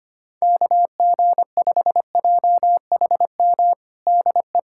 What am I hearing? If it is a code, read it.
Morse code, heard as a keyed tone, about 25 wpm, 700 Hz. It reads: KG5JHM DE
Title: USA Callsign DE Callsign - 6 Characters - 25wpm